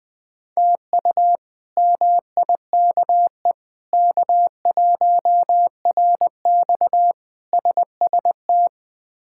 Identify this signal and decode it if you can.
Morse code, heard as a keyed tone, about 21 wpm, 700 Hz. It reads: TU MIKE K1RX SST
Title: K1USN SST QSO Practice - 20wpm